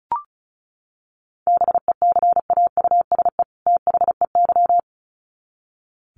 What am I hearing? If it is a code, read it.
Morse code, heard as a keyed tone, about 34 wpm, 700 Hz. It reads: BECAUSE THEY
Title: Sets of 2 Words 35wpm